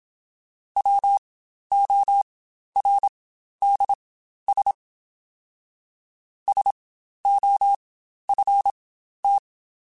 Morse code, heard as a keyed tone, about 27 wpm, spaced as for 12 wpm, 800 Hz. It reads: WORDS SOFT
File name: Spaced / Chuckles Spaced-mp3s / SoftWords sp @25